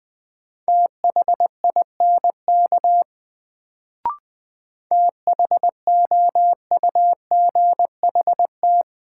Morse code, heard as a keyed tone, about 20 wpm, 700 Hz. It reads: THINK THOUGHT
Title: Sets of 2 Words - New Word Drill - 20wpm